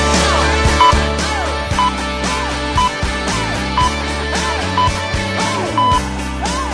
Radio time signal (time pips). It is heard from 0.8 to 6.0 s.